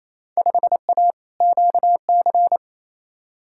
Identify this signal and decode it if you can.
Morse code, heard as a keyed tone, about 28 wpm, 700 Hz. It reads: 5A QC
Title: Field Day Exchanges Encore - 28wpm